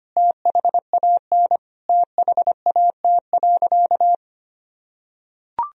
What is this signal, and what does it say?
Morse code, heard as a keyed tone, about 25 wpm, 700 Hz. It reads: THAN THAT.
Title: Sentences from Top 200 Words and Abbreviations 25wpm